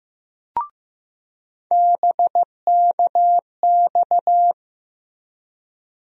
Morse code, heard as a keyed tone, about 15 wpm, 700 Hz. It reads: BKX
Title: USA Call Sign Postfixes - 3 Letters 15wpm